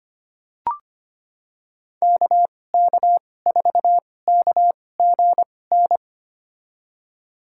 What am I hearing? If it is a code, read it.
Morse code, heard as a keyed tone, about 25 wpm, spaced as for 18 wpm, 700 Hz. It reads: KK4KGN
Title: USA Call Signs - 6 Characters - 2x Character Spacing - 25wpm